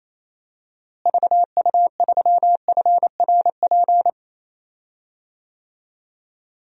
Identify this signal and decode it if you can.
Morse code, heard as a keyed tone, about 28 wpm, 700 Hz. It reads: VU3FRP
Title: DXCC-100 Call Signs - 6 Characters - Repeated 3x - 28wpm